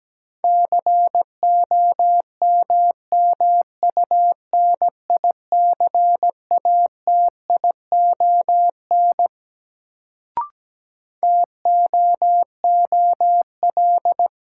Morse code, heard as a keyed tone, about 17 wpm, 700 Hz. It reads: COMMUNICATION TOOL
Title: Sentences from Top 1000 Words - Review Part 6 - New Word Drill - 17wpm